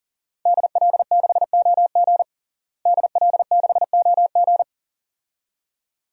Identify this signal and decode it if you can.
Morse code, heard as a keyed tone, about 40 wpm, 700 Hz. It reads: DL6OG DL6OG
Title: DXCC-100 Call Signs - 5 Characters - Repeated 2x - 40wpm